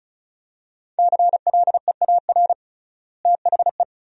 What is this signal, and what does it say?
Morse code, heard as a keyed tone, about 36 wpm, 700 Hz. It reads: CLEAR THE
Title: 3x Word-Spacing - Sentences from Top 500 Words - 35wpm